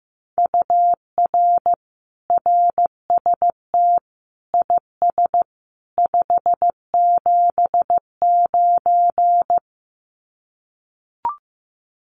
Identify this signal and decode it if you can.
Morse code, heard as a keyed tone, about 15 wpm, 700 Hz. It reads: UR RST IS 579
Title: Common QSO Phrases 15wpm